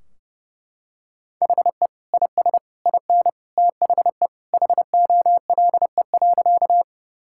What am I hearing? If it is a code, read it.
Morse code, heard as a keyed tone, about 30 wpm, 700 Hz. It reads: HE IS IN THE HOLE.
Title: Sentences from Top 1600 Words 30wpm